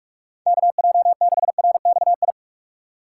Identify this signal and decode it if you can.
Morse code, heard as a keyed tone, about 45 wpm, 700 Hz. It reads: KJ6RXI